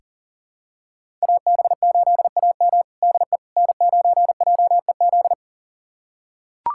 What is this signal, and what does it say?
Morse code, heard as a keyed tone, about 40 wpm, 700 Hz. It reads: AB8AM DE N9JEZ
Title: USA Callsign DE Callsign - 5 Characters - 40wpm